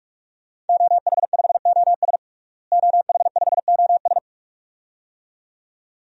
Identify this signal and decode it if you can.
Morse code, heard as a keyed tone, about 46 wpm, 700 Hz. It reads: OH5OS OH5OS
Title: DXCC-100 Call Signs - 5 Characters - Repeated 2x - 45wpm